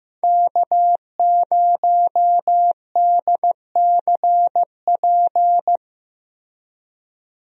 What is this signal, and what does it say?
Morse code, heard as a keyed tone, about 15 wpm, 700 Hz. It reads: K0DCP
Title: USA Call Signs - 5 Characters - Repeated 2x - 15wpm